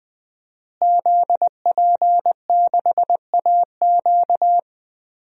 Morse code, heard as a keyed tone, about 20 wpm, 700 Hz. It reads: ZP6AQ
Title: DXCC-100 Call Signs - 5 Characters - Repeated 2x - 20wpm